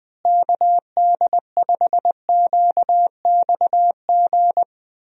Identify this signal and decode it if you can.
Morse code, heard as a keyed tone, about 20 wpm, 700 Hz. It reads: KD5QXG
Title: USA Call Signs - 6 Characters - Repeated 3x - 20wpm